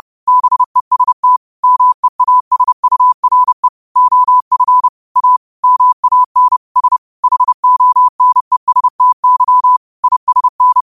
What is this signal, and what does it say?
Morse code, heard as a keyed tone, about 30 wpm, 1000 Hz. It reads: BEST MEASURE OF A MAN S HONESTY ISN
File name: qotdfiles / 30wpm